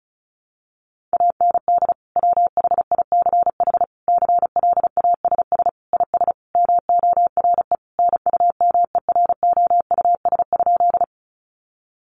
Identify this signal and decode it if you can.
Morse code, heard as a keyed tone, about 34 wpm, 700 Hz. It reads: AND WHICH CLASS IS MORE NUMEROUS?
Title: Sentences from Top 1400 Words 35wpm